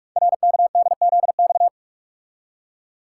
Morse code, heard as a keyed tone, about 44 wpm, 700 Hz. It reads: RKDZX
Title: Code Groups - 5 Letters - 45wpm